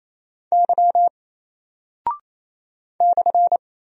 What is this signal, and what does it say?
Morse code, heard as a keyed tone, about 27 wpm, 700 Hz. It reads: Y /
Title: Single Character (Letter:Number:Punctuation:Prosign) Alt 28wpm